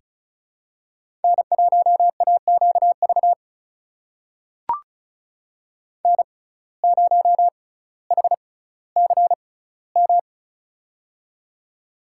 Morse code, heard as a keyed tone, about 36 wpm, spaced as for 18 wpm, 700 Hz. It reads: N1AQV N0HCM
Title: USA Call Signs - 5 Characters - 6x Character Spacing - 35wpm